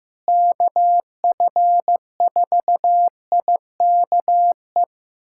Morse code, heard as a keyed tone, about 15 wpm, 700 Hz. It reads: KF4IKE
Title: USA Callsign DE Callsign 15wpm